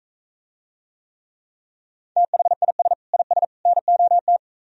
Morse code, heard as a keyed tone, about 45 wpm, 700 Hz. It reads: THIS IS NOT